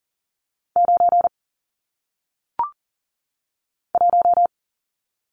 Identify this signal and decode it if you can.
Morse code, heard as a keyed tone, about 42 wpm, 700 Hz. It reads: 9 1
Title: Numbers - 1-Digit 40wpm